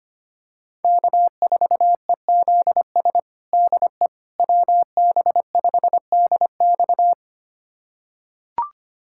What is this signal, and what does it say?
Morse code, heard as a keyed tone, about 25 wpm, 700 Hz. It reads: K4EZS DE WB5DX
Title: USA Callsign DE Callsign - 5 Characters - 25wpm